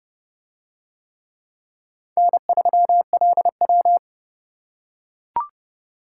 Morse code, heard as a keyed tone, about 28 wpm, 700 Hz. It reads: N3LW